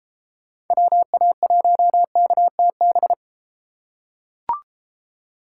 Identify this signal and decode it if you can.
Morse code, heard as a keyed tone, about 31 wpm, 700 Hz. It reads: WA1KTB